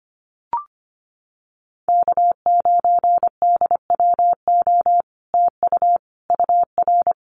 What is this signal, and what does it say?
Morse code, heard as a keyed tone, about 25 wpm, 700 Hz. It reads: K9DWO TU UR
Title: Common QSO Phrases Encore 25wpm